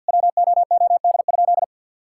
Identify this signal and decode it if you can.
Morse code, heard as a keyed tone, about 50 wpm, 700 Hz. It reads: WOOD?
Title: Sentences from Top 1600 Words 50wpm